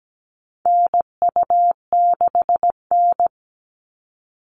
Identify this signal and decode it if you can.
Morse code, heard as a keyed tone, about 17 wpm, 700 Hz. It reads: NU6N
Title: Call Signs Repeated 3x - 17wpm